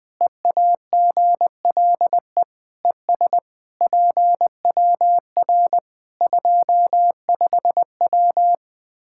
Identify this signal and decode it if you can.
Morse code, heard as a keyed tone, about 20 wpm, 700 Hz. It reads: EAGLE ES PWR 25W
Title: Practice QSO - Exchange 2 - Info - 20wpm